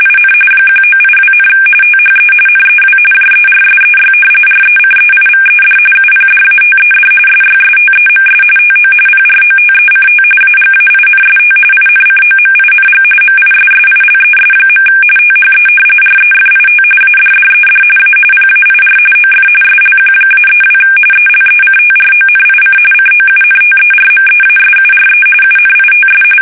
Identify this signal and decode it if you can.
RTTY, about 45.45 baud. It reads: IN 1982, THE NEW JERSEY RECEIVED UPDATED SYSTEMS TO PROTECT AGAINST INCOMING MISSILES. THE REFIT ADDED A PAIR OF SLQ-32 ELECTRONIC COUNTERMEASURE UNITS
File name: fsk850 / battleshipnj